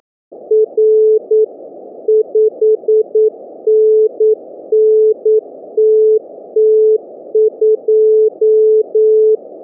Morse code, heard as a keyed tone, about 9 wpm, 430 Hz. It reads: R 5NN TT2